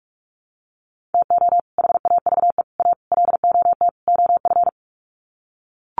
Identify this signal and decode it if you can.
Morse code, heard as a keyed tone, about 44 wpm, 700 Hz. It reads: TO HAVE A LOT OF
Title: Sets of 5 Words Encore 45wpm